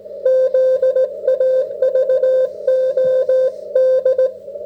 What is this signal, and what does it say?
Morse code, heard as a keyed tone, about 17 wpm, 530 Hz. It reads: ZAVOD